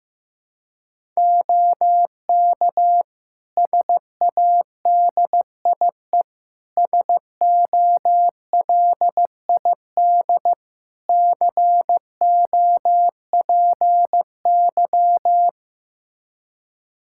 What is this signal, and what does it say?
Morse code, heard as a keyed tone, about 16 wpm, 700 Hz. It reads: OK SADIE SOLID COPY
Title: Practice QSO - Exchange 3 - More Info - 15wpm